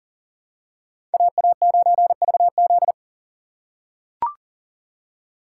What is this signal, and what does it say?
Morse code, heard as a keyed tone, about 40 wpm, 700 Hz. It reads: AA9VZ
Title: USA Call Signs - 5 Characters - 2x Character Spacing - 40wpm